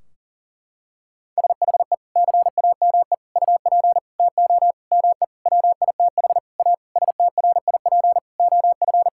Morse code, heard as a keyed tone, about 40 wpm, 700 Hz. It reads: SHE CAME UP TO ME WITH A STRIP OF
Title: Sentences from Top 1600 Words 40wpm